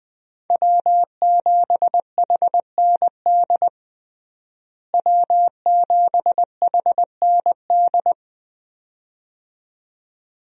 Morse code, heard as a keyed tone, about 21 wpm, 700 Hz. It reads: W7HND W7HND
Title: USA Call Signs - 5 Characters - Repeated 2x - 20wpm